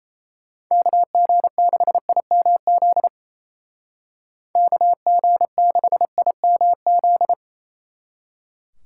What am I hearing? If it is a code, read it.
Morse code, heard as a keyed tone, about 33 wpm, 700 Hz. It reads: KG6IMZ KG6IMZ
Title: Speed Racing - USA 2x3 Call Signs 28wpm